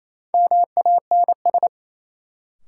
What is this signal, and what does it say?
Morse code, heard as a keyed tone, about 28 wpm, 700 Hz. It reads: MANS
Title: Sentences from Top 1000 Words - Review Part 6 - New Word Drill - 28wpm